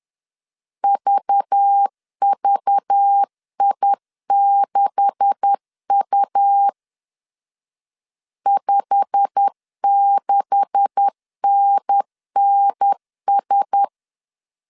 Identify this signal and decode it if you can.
Morse code, heard as a keyed tone, about 10 wpm, 800 Hz. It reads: VVI6U 56NNS